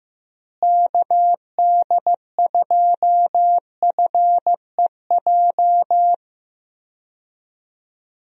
Morse code, heard as a keyed tone, about 15 wpm, 700 Hz. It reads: KD2FEJ
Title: USA Call Signs - 6 Characters - Repeated 2x - 15wpm